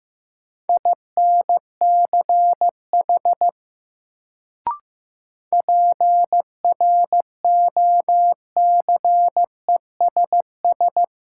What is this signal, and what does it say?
Morse code, heard as a keyed tone, about 15 wpm, 700 Hz. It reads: INCH PROCESS
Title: Sentences from Top 1000 Words - Review Part 3 - New Word Drill - 15wpm